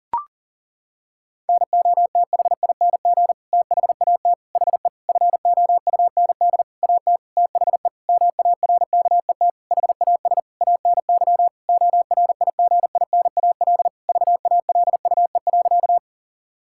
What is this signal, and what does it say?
Morse code, heard as a keyed tone, about 40 wpm, 700 Hz. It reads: NOTHING THAT HE FOUND AT THE MARKET HAS ANY ORIGINAL VALUE.
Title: Sentences from Top 1000 Words - Review Part 6 - 40wpm